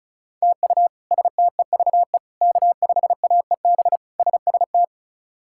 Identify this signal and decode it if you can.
Morse code, heard as a keyed tone, about 36 wpm, 700 Hz. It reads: TU STEVE K5AEB SST
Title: K1USN SST QSO Practice - 35wpm